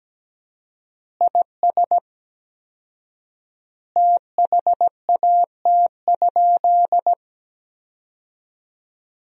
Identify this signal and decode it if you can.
Morse code, heard as a keyed tone, about 17 wpm, 700 Hz. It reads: IS THAT?
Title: 4x Word-Spacing - Sentences from Top 100 Words - 17wpm